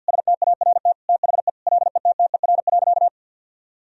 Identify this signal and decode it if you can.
Morse code, heard as a keyed tone, about 48 wpm, 700 Hz. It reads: START THE LETTER.
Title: Sentences from Top 1000 Words - Review Part 1 - 50wpm